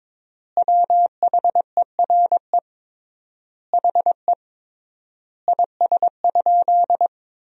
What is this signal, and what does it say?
Morse code, heard as a keyed tone, about 22 wpm, 700 Hz. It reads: WHERE HE IS?
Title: 3x Word-Spacing - Sentences from Top 300 Words - 22wpm